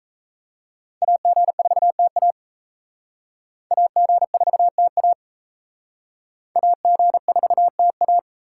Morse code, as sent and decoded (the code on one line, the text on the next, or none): .- --. ....- - .- / .- --. ....- - .- / .- --. ....- - .-
AG4TA AG4TA AG4TA